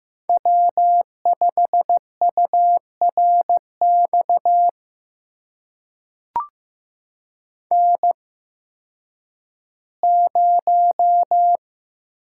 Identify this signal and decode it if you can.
Morse code, heard as a keyed tone, about 15 wpm, spaced as for 10 wpm, 700 Hz. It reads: W5URX N0
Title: USA Call Signs - 5 Characters - 8x Character Spacing - 15wpm